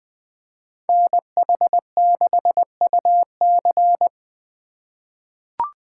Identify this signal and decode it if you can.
Morse code, heard as a keyed tone, about 20 wpm, 700 Hz. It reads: NH6UC
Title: USA Call Signs - 2x Character Spacing - 20wpm